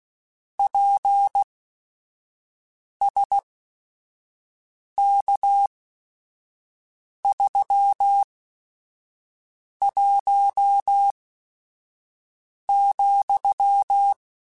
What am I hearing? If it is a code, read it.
Morse code, heard as a keyed tone, about 16 wpm, spaced as for 5 wpm, 800 Hz. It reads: PSK31,